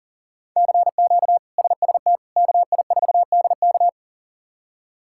Morse code, heard as a keyed tone, about 40 wpm, 700 Hz. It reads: CQ SST KI4DK
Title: K1USN SST QSO Practice - 40wpm